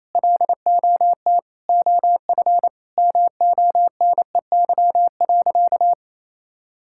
Morse code, heard as a keyed tone, about 28 wpm, 700 Hz. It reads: LOT OF MONEY.
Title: Sentences from Top 1000 Words - Review Part 5 - 28wpm